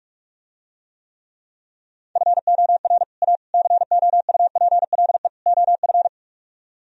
Morse code, heard as a keyed tone, about 45 wpm, 700 Hz. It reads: FOR A COUPLE OF